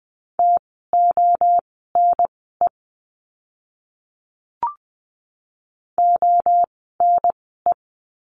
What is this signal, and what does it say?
Morse code, heard as a keyed tone, about 20 wpm, spaced as for 15 wpm, 700 Hz. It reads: TONE ONE
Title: 068 Lesson 12f - Letter d - Words 20wpm